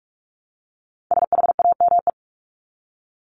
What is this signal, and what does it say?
Morse code, heard as a keyed tone, about 44 wpm, 700 Hz. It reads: SHAME
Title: Top 1751-2000 Words 45wpm